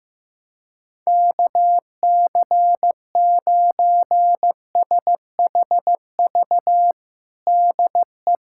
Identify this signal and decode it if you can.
Morse code, heard as a keyed tone, about 15 wpm, 700 Hz. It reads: KC9SHV DE
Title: USA Callsign DE Callsign - 6 Characters - 15wpm